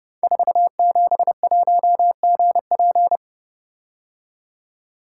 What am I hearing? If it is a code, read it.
Morse code, heard as a keyed tone, about 30 wpm, 700 Hz. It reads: 471GP
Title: Code Groups - 5 Letter:Numbers - 30wpm